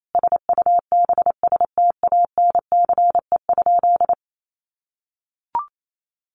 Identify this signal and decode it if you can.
Morse code, heard as a keyed tone, about 28 wpm, 700 Hz. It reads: SUBSTANCE?
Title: Sentences from Top 1500 Words 28wpm